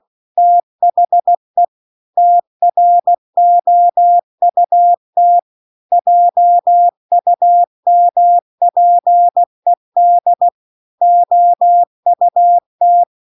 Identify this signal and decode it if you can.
Morse code, heard as a keyed tone, about 16 wpm, 700 Hz. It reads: THE TROUT JUMPED OUT